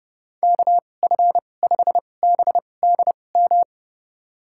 Morse code, heard as a keyed tone, about 30 wpm, spaced as for 22 wpm, 700 Hz. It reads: KF5BDM
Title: USA Call Signs - 6 Characters - 2x Character Spacing - 30wpm